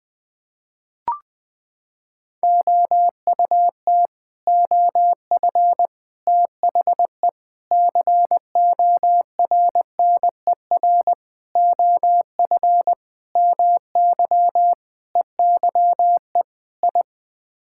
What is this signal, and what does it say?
Morse code, heard as a keyed tone, about 20 wpm, 700 Hz. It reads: OUT OF THE CORNER OF MY EYE I
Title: Sets of 8 Words - 20wpm